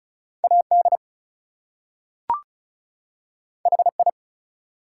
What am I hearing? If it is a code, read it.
Morse code, heard as a keyed tone, about 36 wpm, 700 Hz. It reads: AD HI
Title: Two Letter Words - 35wpm